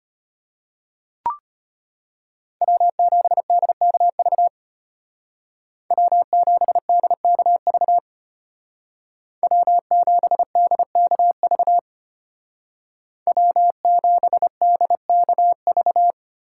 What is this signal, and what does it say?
Morse code, heard as a keyed tone, about 33 wpm, 700 Hz. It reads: W7DKV W7DKV W7DKV W7DKV